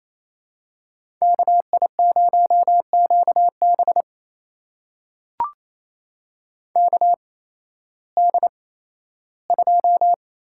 Morse code, heard as a keyed tone, about 28 wpm, spaced as for 15 wpm, 700 Hz. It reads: KI0QB KD2
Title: USA Call Signs - 8x Character Spacing - 28wpm